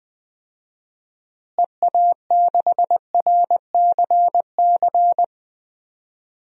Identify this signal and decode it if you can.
Morse code, heard as a keyed tone, about 20 wpm, 700 Hz. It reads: EA6RCC
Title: DXCC-100 Call Signs - 6 Characters - Repeated 3x - 20wpm